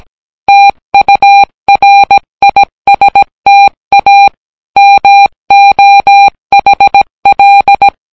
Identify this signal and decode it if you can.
Morse code, heard as a keyed tone, about 17 wpm, 800 Hz. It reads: TURISTA MOHL